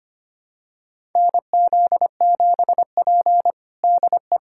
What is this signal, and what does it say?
Morse code, heard as a keyed tone, about 25 wpm, 700 Hz. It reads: NZ7P DE
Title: USA Callsign DE Callsign - 4 Characters - 25wpm